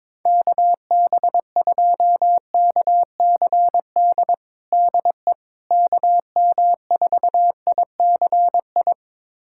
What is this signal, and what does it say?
Morse code, heard as a keyed tone, about 22 wpm, 700 Hz. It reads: KB2KCD DE KM4ICI
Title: USA Callsign DE Callsign 22wpm